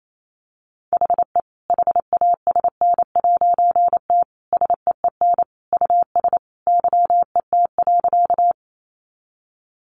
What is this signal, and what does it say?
Morse code, heard as a keyed tone, about 28 wpm, 700 Hz. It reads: HE HASN'T SEEN US YET.
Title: A Sound Of Thunder 28wpm